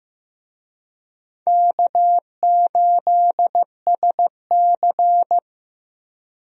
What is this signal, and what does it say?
Morse code, heard as a keyed tone, about 15 wpm, 700 Hz. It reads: K8SC
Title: USA Call Signs - 4 Characters - 15wpm